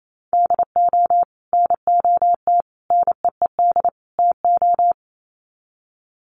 Morse code, heard as a keyed tone, about 28 wpm, 700 Hz. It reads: DO NOT NEED TO
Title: Sets of 4 Words 28wpm